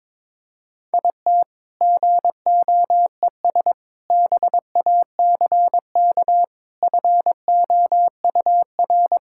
Morse code, heard as a keyed tone, about 22 wpm, 690 Hz. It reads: IT GOES BACK FOUR
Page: morse-code-ninja-practice-22wpm